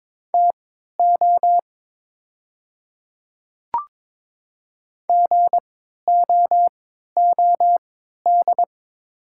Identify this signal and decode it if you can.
Morse code, heard as a keyed tone, about 22 wpm, spaced as for 13 wpm, 700 Hz. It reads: TO GOOD
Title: 3x Character Spacing - Top 200 Words - 22wpm